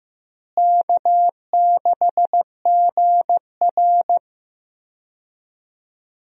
Morse code, heard as a keyed tone, about 15 wpm, 700 Hz. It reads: K6GR
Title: USA Call Signs - 4 Characters - Repeated 3x - 15wpm